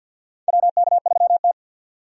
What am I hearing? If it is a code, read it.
Morse code, heard as a keyed tone, about 48 wpm, 700 Hz. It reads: WK3T
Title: USA Call Signs - 4 Characters - 50wpm